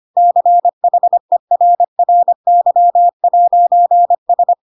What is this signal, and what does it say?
Morse code, heard as a keyed tone, about 25 wpm, 700 Hz. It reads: CHERRY'S
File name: Pinocchio / Pinocchio ch1 25 25 ik1hge